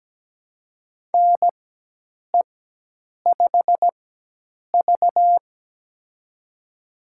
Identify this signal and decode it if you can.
Morse code, heard as a keyed tone, about 17 wpm, spaced as for 8 wpm, 700 Hz. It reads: NE5V
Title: USA Call Signs - 4 Characters - 4x Character Spacing - 17wpm